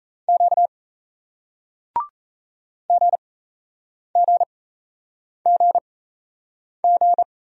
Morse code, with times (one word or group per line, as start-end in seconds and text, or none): 0.3-0.7 Q
2.9-7.2 GGGG